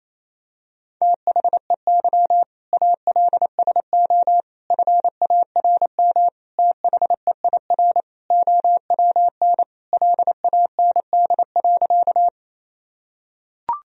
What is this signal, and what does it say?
Morse code, heard as a keyed tone, about 28 wpm, 700 Hz. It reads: THEY ALSO FARM THEIR OWN LAND.
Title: Sentences from Top 1000 Words - Review Part 2 - 28wpm